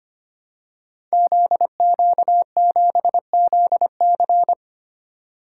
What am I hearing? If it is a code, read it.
Morse code, heard as a keyed tone, about 25 wpm, 700 Hz. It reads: ZQ7ZC
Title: Code Groups - 5 Letter:Numbers - 25wpm